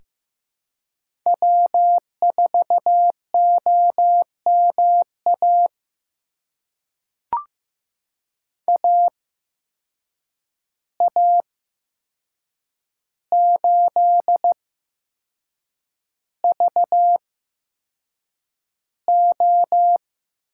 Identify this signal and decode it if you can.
Morse code, heard as a keyed tone, about 15 wpm, spaced as for 6 wpm, 700 Hz. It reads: W4OMA AA8VO
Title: USA Call Signs - 5 Characters - 8x Character Spacing - 15wpm